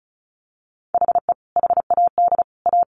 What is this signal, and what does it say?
Morse code, heard as a keyed tone, about 34 wpm, 700 Hz. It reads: HE HAD A
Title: Sets of 3 Words 35wpm